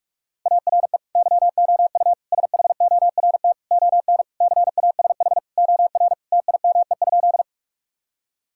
Morse code, heard as a keyed tone, about 46 wpm, 690 Hz. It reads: ARE YOU SHORT ON CASH OR TIME?
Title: Sentences from Top 1000 Words - Review Part 5 - 45wpm